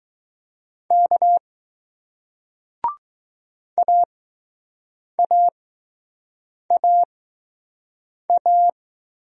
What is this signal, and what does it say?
Morse code, heard as a keyed tone, about 20 wpm, spaced as for 6 wpm, 700 Hz. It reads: K AAAA